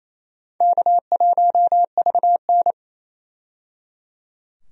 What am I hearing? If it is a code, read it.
Morse code, heard as a keyed tone, about 28 wpm, 700 Hz. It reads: K1VN
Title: USA Call Signs - 4 Characters - 28wpm